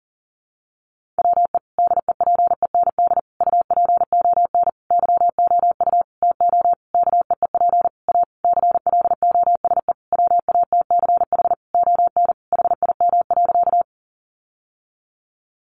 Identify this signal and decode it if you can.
Morse code, heard as a keyed tone, about 40 wpm, 700 Hz. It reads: WE DEPEND UPON YOU TO KEEP A CLOSE WATCH ON HIM.
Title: Sentences from Top 1300 Words 40wpm